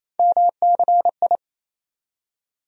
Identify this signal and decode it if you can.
Morse code, heard as a keyed tone, about 28 wpm, 700 Hz. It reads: MCI
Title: USA Call Sign Postfixes - 3 Letters 28wpm